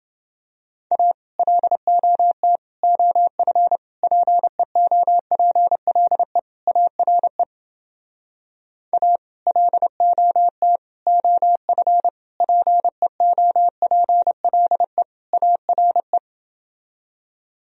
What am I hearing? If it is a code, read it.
Morse code, heard as a keyed tone, about 29 wpm, 700 Hz. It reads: A LOT OF PEOPLE ARE A LOT OF PEOPLE ARE